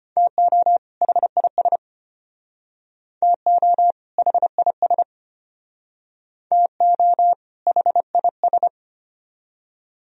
Morse code, heard as a keyed tone, about 31 wpm, 700 Hz. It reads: TO HIS TO HIS TO HIS
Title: Speed-Racing - Sets of 2 Words 25wpm